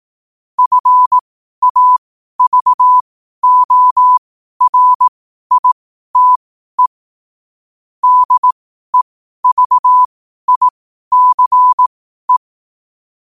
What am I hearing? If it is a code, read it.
Morse code, heard as a keyed tone, about 18 wpm, spaced as for 12 wpm, 1000 Hz. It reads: FAVORITE DEVICE
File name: qotdfiles / 12wpm